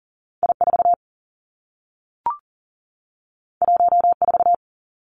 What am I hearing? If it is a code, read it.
Morse code, heard as a keyed tone, about 41 wpm, 700 Hz. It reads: I4 14
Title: Two Challenging Letter-Number Combinations 40wpm